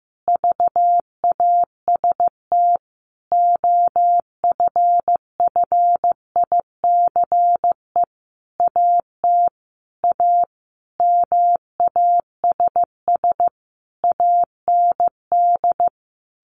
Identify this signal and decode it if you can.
Morse code, heard as a keyed tone, about 15 wpm, 700 Hz. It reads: VAST OFFICE AT A MASS AND
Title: A Sound Of Thunder 15wpm